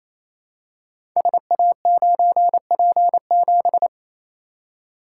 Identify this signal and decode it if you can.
Morse code, heard as a keyed tone, about 28 wpm, 700 Hz. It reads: SA9P7
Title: Code Groups - 5 Letter:Numbers - 28wpm